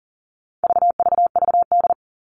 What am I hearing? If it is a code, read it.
Morse code, heard as a keyed tone, about 40 wpm, 700 Hz. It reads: VVVD
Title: All Cut Numbers - 4-Digit 40wpm